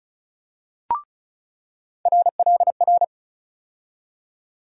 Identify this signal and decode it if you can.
Morse code, heard as a keyed tone, about 34 wpm, 700 Hz. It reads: RLR
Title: USA Call Sign Postfixes - 3 Letters 35wpm